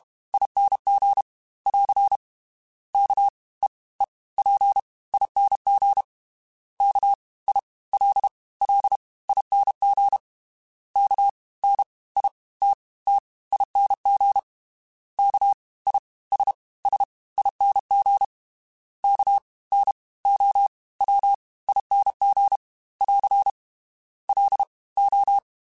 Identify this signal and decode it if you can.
Morse code, heard as a keyed tone, about 32 wpm, 800 Hz. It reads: ING <AR> K E E P ING K I L L ING K N I T T ING K I S S ING K N O W ING <AR> L O
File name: Special INGs / Special INGs-mp3s / Fast ing @30